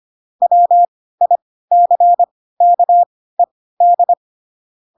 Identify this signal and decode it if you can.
Morse code, heard as a keyed tone, about 25 wpm, spaced as for 16 wpm, 700 Hz. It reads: WICKED
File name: Pinocchio / Pinocchio ch3 10 25 ik1hge